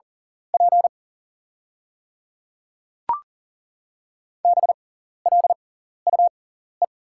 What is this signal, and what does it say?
Morse code, heard as a keyed tone, about 37 wpm, spaced as for 14 wpm, 690 Hz. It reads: P BLUE
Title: 6x Character Spacing - Top 500 Words - 40wpm